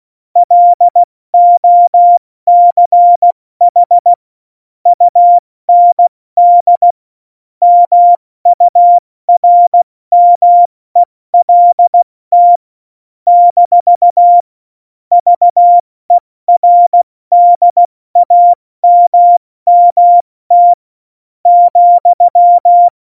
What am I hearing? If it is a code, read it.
Morse code, heard as a keyed tone, about 16 wpm, spaced as for 14 wpm, 700 Hz. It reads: LOCH UND MURMELT - VERDAMMT,
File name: Morsekurs / Witze / Witz 14. August 2018, 70 F80